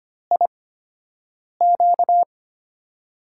Morse code, heard as a keyed tone, about 25 wpm, spaced as for 7 wpm, 700 Hz. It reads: IQ
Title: USA Call Signs - 6 Characters - 8x Character Spacing - 25wpm